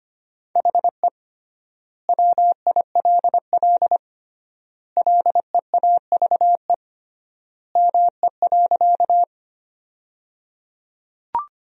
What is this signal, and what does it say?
Morse code, heard as a keyed tone, about 25 wpm, 700 Hz. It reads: HE WILL LEAVE ME.
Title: 3x Word-Spacing - Sentences from Top 400 Words - 25wpm